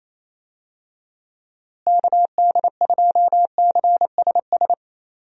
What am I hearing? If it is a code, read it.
Morse code, heard as a keyed tone, about 28 wpm, 700 Hz. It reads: KD2CSS
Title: USA Call Signs - 6 Characters - Repeated 2x - 28wpm